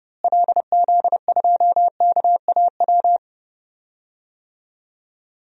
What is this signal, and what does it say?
Morse code, heard as a keyed tone, about 30 wpm, 700 Hz. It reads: LZ2KAW
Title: DXCC-100 Call Signs - 6 Characters - Repeated 2x - 30wpm